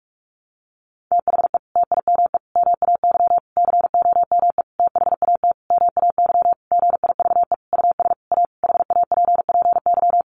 Morse code, heard as a keyed tone, about 44 wpm, 700 Hz. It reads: THE TIME MAY COME THAT MAY GIVE US A HAPPY
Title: Sentences from Top 1100 Words 45wpm